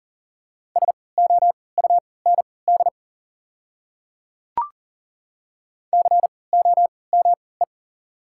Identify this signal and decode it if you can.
Morse code, heard as a keyed tone, about 40 wpm, spaced as for 23 wpm, 690 Hz. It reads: SOUND COME
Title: 3x Character Spacing - Top 100 Words - 40wpm